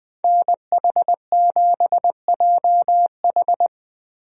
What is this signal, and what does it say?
Morse code, heard as a keyed tone, about 20 wpm, 700 Hz. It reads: NH7JH
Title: USA Call Signs - 5 Characters - Repeated 2x - 20wpm